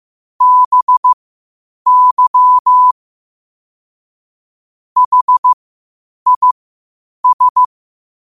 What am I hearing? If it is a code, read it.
Morse code, heard as a keyed tone, about 15 wpm, spaced as for 8 wpm, 1000 Hz. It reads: BY HIS
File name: qotdfiles / 8wpm